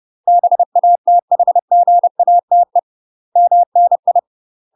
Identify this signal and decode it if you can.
Morse code, heard as a keyed tone, about 30 wpm, 700 Hz. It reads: BATHGATE MNI
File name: UK MASTER QSOs TRAINING / 30WPM / UK MASTER QSOs 0044